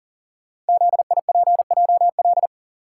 Morse code, heard as a keyed tone, about 40 wpm, 700 Hz. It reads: ZIPJL
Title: Code Groups - 5 Letter:Numbers - 40wpm